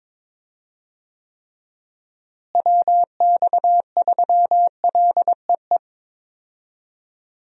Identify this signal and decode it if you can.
Morse code, heard as a keyed tone, about 22 wpm, 700 Hz. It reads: WX3LEE
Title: USA Call Signs - 6 Characters - Repeated 2x - 22wpm